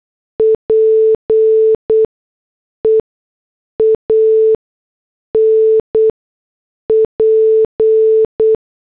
Morse code, heard as a keyed tone, about 8 wpm, spaced as for 6 wpm, 440 Hz. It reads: PEAN P